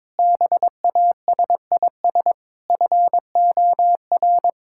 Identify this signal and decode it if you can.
Morse code, heard as a keyed tone, about 21 wpm, 700 Hz. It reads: BASIS FOR
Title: Sentences from Top 1000 Words - Review Part 5 - 22wpm